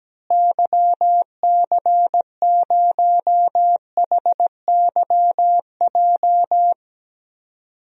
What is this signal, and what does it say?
Morse code, heard as a keyed tone, about 17 wpm, 700 Hz. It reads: YC0HYJ
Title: DXCC-100 Call Signs - 6 Characters - Repeated 2x - 17wpm